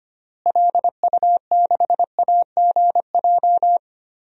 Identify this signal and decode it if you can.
Morse code, heard as a keyed tone, about 25 wpm, 690 Hz. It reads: LU6AGJ
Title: DXCC-100 Call Signs - 6 Characters - Repeated 3x - 25wpm